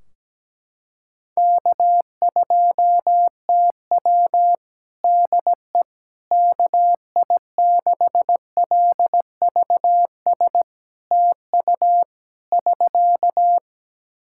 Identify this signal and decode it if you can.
Morse code, heard as a keyed tone, about 17 wpm, 700 Hz. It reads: K2TW DE KI6LVS TU <SK>
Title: Practice QSO - Exchange 4 - Ending - 17wpm